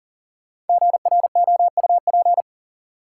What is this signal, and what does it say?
Morse code, heard as a keyed tone, about 40 wpm, 700 Hz. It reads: GROUP
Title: Sentences from Top 1000 Words - Review Part 1 - New Word Drill - 40wpm